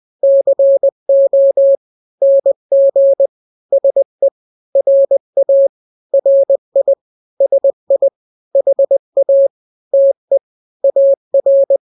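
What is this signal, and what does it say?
Morse code, heard as a keyed tone, about 20 wpm, 550 Hz. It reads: CO NG SE RA RI SI HA TE AR
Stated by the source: Morse drill of two-letter n-grams at 20/18 WPM (Farnsworth), from KK7RR.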